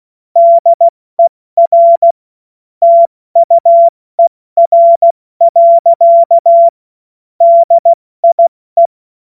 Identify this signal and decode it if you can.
Morse code, heard as a keyed tone, about 16 wpm, 680 Hz. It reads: DER TUER. DIE
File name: Morsekurs / Witze / Witz 24. Januar 2023 70 F80